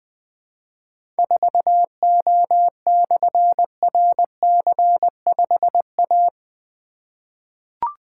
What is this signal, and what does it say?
Morse code, heard as a keyed tone, about 20 wpm, 700 Hz. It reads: 4O/RC5A
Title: Difficult Call Signs 20wpm